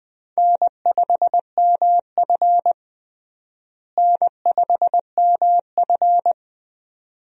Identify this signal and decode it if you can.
Morse code, heard as a keyed tone, about 20 wpm, 700 Hz. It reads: N5MF N5MF
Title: USA Call Signs - 4 Characters - Repeated 2x - 20wpm